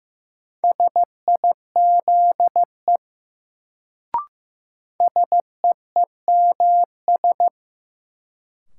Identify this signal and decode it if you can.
Morse code, heard as a keyed tone, about 15 wpm, 700 Hz. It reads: SIZE SEEMS
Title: Sets of 3 Words - New Word Drill - 15wpm